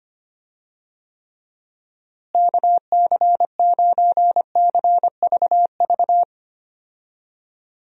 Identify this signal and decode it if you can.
Morse code, heard as a keyed tone, about 25 wpm, 700 Hz. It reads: KC9CVV
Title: USA Call Signs - 6 Characters - Repeated 3x - 25wpm